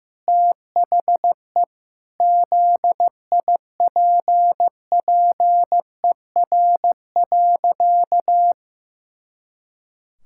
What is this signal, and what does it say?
Morse code, heard as a keyed tone, about 15 wpm, 700 Hz. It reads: THE ZIPPER.
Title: Pangram Challenge v2 - 15wpm